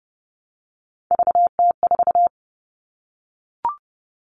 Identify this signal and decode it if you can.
Morse code, heard as a keyed tone, about 30 wpm, 700 Hz. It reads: VT4
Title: All Cut Numbers - 3-Digit 30wpm